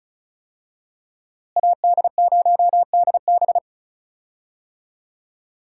Morse code, heard as a keyed tone, about 34 wpm, 700 Hz. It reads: AD0DB
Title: USA Call Signs - 5 Characters - Repeated 2x - 35wpm